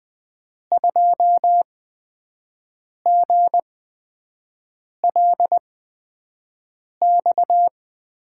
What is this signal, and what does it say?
Morse code, heard as a keyed tone, about 20 wpm, spaced as for 5 wpm, 700 Hz. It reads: 2GLX
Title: USA Call Signs - 5 Characters - 8x Character Spacing - 20wpm